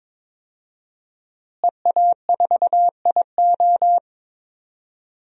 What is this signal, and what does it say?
Morse code, heard as a keyed tone, about 22 wpm, 700 Hz. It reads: EA4IO
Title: DXCC-100 Call Signs - 5 Characters - Repeated 2x - 22wpm